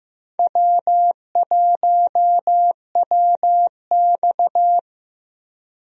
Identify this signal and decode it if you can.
Morse code, heard as a keyed tone, about 15 wpm, 700 Hz. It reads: W1WX
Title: USA Call Signs - 4 Characters - 15wpm